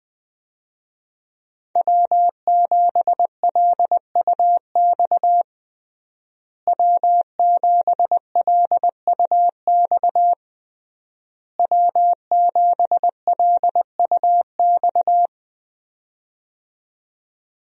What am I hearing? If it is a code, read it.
Morse code, heard as a keyed tone, about 20 wpm, 700 Hz. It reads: W7LUX W7LUX W7LUX
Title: USA Call Signs - 5 Characters - Repeated 3x - 20wpm